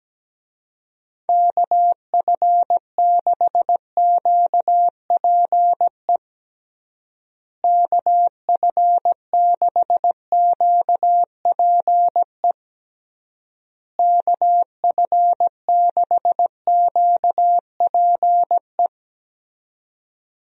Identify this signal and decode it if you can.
Morse code, heard as a keyed tone, about 17 wpm, 700 Hz. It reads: KF6QPE KF6QPE KF6QPE
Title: USA Call Signs - 6 Characters - Repeated 3x - 17wpm